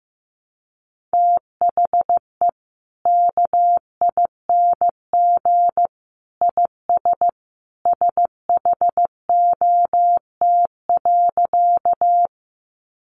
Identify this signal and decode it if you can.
Morse code, heard as a keyed tone, about 15 wpm, 700 Hz. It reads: THE KING IS SHOT.
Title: Sentences from Top 1100 Words 15wpm